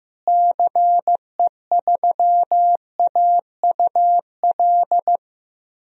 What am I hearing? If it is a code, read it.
Morse code, heard as a keyed tone, about 15 wpm, 700 Hz. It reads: CE3AUL
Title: DXCC-100 Call Signs - 6 Characters - Repeated 2x - 15wpm